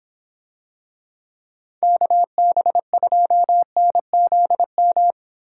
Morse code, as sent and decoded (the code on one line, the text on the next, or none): -.- -... ..--- -. --.. --
KB2NZM